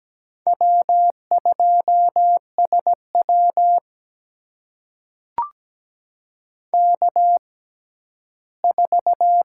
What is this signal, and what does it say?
Morse code, heard as a keyed tone, about 17 wpm, spaced as for 11 wpm, 700 Hz. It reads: W2SW K4
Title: USA Call Signs - 6x Character Spacing - 17wpm